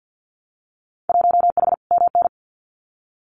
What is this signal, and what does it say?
Morse code, heard as a keyed tone, about 50 wpm, 700 Hz. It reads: 1H MN
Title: Winter Field Day Exchanges Encore 50wpm